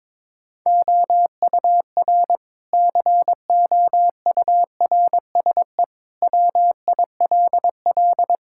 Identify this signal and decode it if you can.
Morse code, heard as a keyed tone, about 22 wpm, 700 Hz. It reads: OUR COURSE WILL
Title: Sentences from Top 1000 Words - Review Part 6 - 22wpm